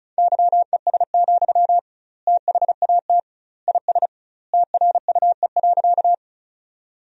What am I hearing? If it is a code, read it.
Morse code, heard as a keyed tone, about 36 wpm, 700 Hz. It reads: YES, THAT IS TRUE.
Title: 2x Word-Spacing - Sentences from Top 500 Words - 35wpm